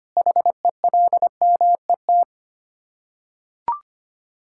Morse code, heard as a keyed tone, about 25 wpm, 700 Hz. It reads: HELMET
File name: practice / 25wpm / Difficult Words with Hints - 25wpm